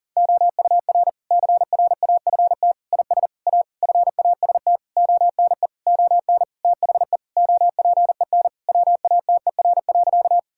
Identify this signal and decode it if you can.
Morse code, heard as a keyed tone, about 40 wpm, 700 Hz. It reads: OUR CRAFT IS A FAST ONE ON THE OPEN WATER.
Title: Sentences from Top 1000 Words - Review Part 6 - 40wpm